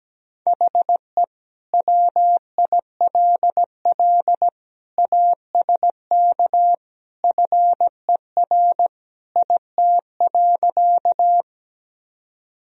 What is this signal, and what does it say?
Morse code, heard as a keyed tone, about 17 wpm, 700 Hz. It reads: HE WILL ASK FER IT.
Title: Sentences from Top 200 Words and Abbreviations 17wpm